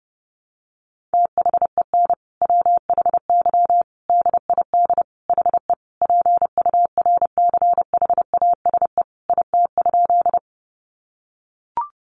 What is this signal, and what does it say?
Morse code, heard as a keyed tone, about 30 wpm, 700 Hz. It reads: THEN WHY DID HE PURCHASE IT?
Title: Sentences from Top 800 Words 30wpm